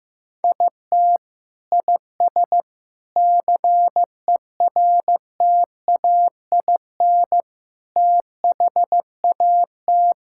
Morse code, heard as a keyed tone, about 15 wpm, 700 Hz. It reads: IT IS CERTAIN THAT
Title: Sentences from Top 1000 Words - Review Part 5 - 15wpm